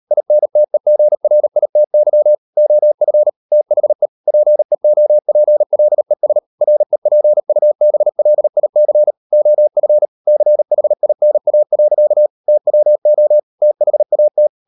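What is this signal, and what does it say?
Morse code, heard as a keyed tone, about 38 wpm, 600 Hz. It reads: INTEGRITY OF THE PEOPLES REPUBLIC OF CHINA. TWO THAT
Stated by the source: Morse code practice at 35–39 wpm sending English text of Ken Follett's novel Never, from CW med Gnister.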